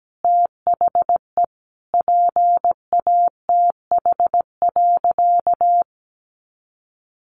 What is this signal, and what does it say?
Morse code, heard as a keyed tone, about 17 wpm, 700 Hz. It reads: THE PATH.
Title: Sentences from Top 1400 Words 17wpm